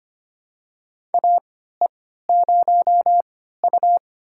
Morse code, heard as a keyed tone, about 25 wpm, spaced as for 14 wpm, 700 Hz. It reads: AE0U
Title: USA Call Signs - 4 Characters - 3x Character Spacing - 25wpm